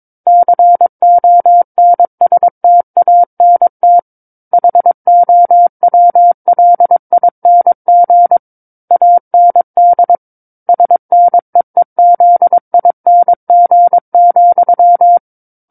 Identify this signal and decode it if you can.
Morse code, heard as a keyed tone, about 22 wpm, 700 Hz. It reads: CONSTANT HOWLING AND SNEEZING,
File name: 06-Chapter-06 22 WPM